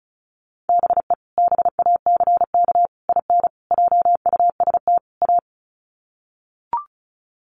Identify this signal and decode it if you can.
Morse code, heard as a keyed tone, about 34 wpm, 700 Hz. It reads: BE BACK IN JUST A
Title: Sets of 5 Words Encore 35wpm